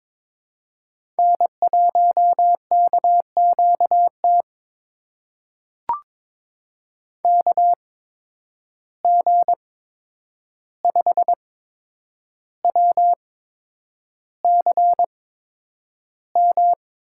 Morse code, heard as a keyed tone, about 23 wpm, spaced as for 9 wpm, 700 Hz. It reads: N1KQT KG5WCM
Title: USA Call Signs - 8x Character Spacing - 22wpm